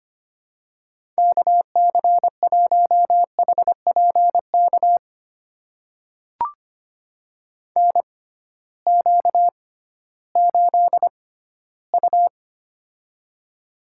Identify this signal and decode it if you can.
Morse code, heard as a keyed tone, about 25 wpm, spaced as for 14 wpm, 700 Hz. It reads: KC1HPK NQ8U
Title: USA Call Signs - 6x Character Spacing - 25wpm